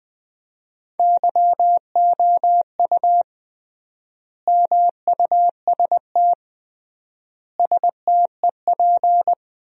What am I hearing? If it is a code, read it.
Morse code, heard as a keyed tone, about 20 wpm, 700 Hz. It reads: YOU MUST STEP
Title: 3x Word-Spacing - Sentences from Top 400 Words - 20wpm